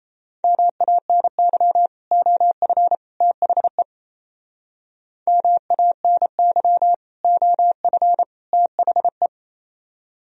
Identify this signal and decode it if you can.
Morse code, heard as a keyed tone, about 34 wpm, 700 Hz. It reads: MANY OF THE MANY OF THE
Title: Speed-Racing - Sets of 3 Words 28wpm